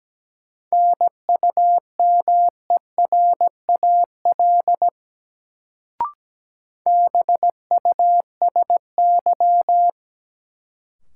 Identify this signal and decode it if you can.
Morse code, heard as a keyed tone, about 17 wpm, 700 Hz. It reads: NUMERAL BUSY
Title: Sentences from Top 1000 Words - Review Part 2 - New Word Drill - 17wpm